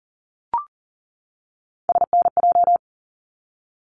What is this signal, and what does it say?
Morse code, heard as a keyed tone, about 40 wpm, 700 Hz. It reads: SNJ
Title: ARRL Sections 40wpm